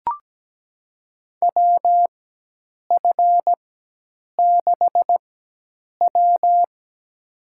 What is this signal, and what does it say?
Morse code, heard as a keyed tone, about 17 wpm, spaced as for 8 wpm, 700 Hz. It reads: WF6W
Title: USA Call Signs - 4x Character Spacing - 17wpm